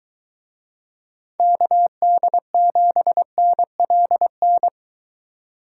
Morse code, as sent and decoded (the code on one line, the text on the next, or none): -.- -.. --... -. .-.. -.
KD7NLN